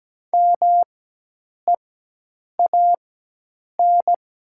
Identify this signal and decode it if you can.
Morse code, heard as a keyed tone, about 17 wpm, spaced as for 8 wpm, 700 Hz. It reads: MEAN
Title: 4x Character Spacing - Top 300 Words - 17wpm